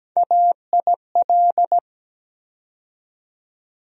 Morse code, heard as a keyed tone, about 17 wpm, 700 Hz. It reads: AIL
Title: USA Call Sign Postfixes - 3 Letters 17wpm